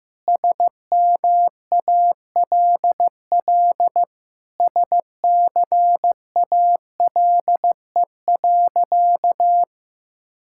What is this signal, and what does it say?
Morse code, heard as a keyed tone, about 15 wpm, 700 Hz. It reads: SMALL SCALE.
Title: Sentences from Top 1000 Words - Review Part 5 - 15wpm